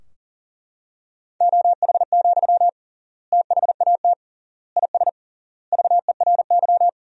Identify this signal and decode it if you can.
Morse code, heard as a keyed tone, about 40 wpm, 700 Hz. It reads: OH, THAT IS VERY
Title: 3x Word-Spacing - Sentences from Top 500 Words - 40wpm